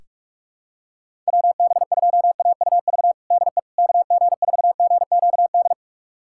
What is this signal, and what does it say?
Morse code, heard as a keyed tone, about 45 wpm, 700 Hz. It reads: WB2AUV DE KG4GQD
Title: USA Callsign DE Callsign - 6 Characters - 45wpm